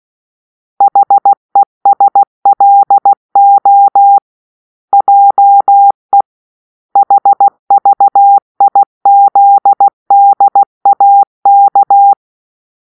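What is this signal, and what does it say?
Morse code, heard as a keyed tone, about 16 wpm, 800 Hz. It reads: HESLO JE HVIZDAK
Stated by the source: Morse-code cipher made by a player team for the Kachna puzzle game.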